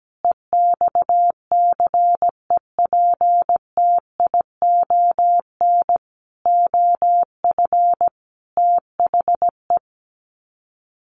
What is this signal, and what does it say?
Morse code, heard as a keyed tone, about 17 wpm, 700 Hz. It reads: EXCEPTION OF THE
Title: Sets of 5 Words Encore 17wpm